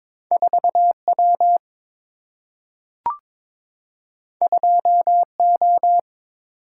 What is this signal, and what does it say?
Morse code, heard as a keyed tone, about 22 wpm, 700 Hz. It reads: 4 W 2 O
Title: Single-Letter Single-Number 22wpm